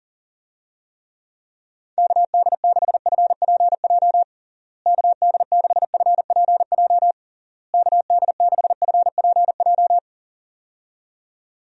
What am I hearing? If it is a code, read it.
Morse code, heard as a keyed tone, about 40 wpm, 700 Hz. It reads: KD6FPJ KD6FPJ KD6FPJ
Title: USA Call Signs - 6 Characters - Repeated 3x - 40wpm